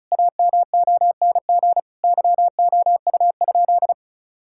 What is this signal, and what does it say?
Morse code, heard as a keyed tone, about 36 wpm, 700 Hz. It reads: AMONG YOU?
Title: Sentences from Top 1600 Words 35wpm